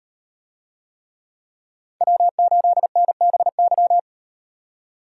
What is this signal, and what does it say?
Morse code, heard as a keyed tone, about 35 wpm, 700 Hz. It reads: W8NBY